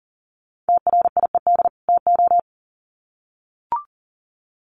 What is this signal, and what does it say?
Morse code, heard as a keyed tone, about 40 wpm, 700 Hz. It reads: TRIED TO
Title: Sets of 2 Words 40wpm